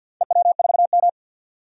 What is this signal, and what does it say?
Morse code, heard as a keyed tone, about 51 wpm, 700 Hz. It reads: EW4M
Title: DXCC-100 Call Signs - 4 Characters - 50wpm